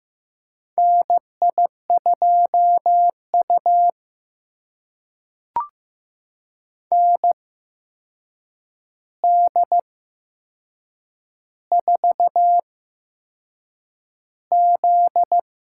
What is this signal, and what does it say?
Morse code, heard as a keyed tone, about 15 wpm, spaced as for 6 wpm, 700 Hz. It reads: NI2U ND4Z
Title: USA Call Signs - 4 Characters - 8x Character Spacing - 15wpm